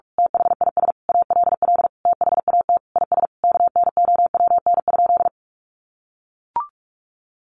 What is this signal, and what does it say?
Morse code, heard as a keyed tone, about 44 wpm, 700 Hz. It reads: THIS ALL THAT IS KNOWN?
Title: Sentences from Top 600 Words 45wpm